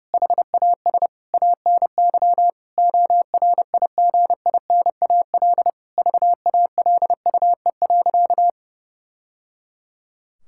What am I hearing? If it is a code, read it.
Morse code, heard as a keyed tone, about 30 wpm, 700 Hz. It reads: HAS ANY ORIGINAL VALUE.
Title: Sentences from Top 1000 Words - Review Part 6 - 30wpm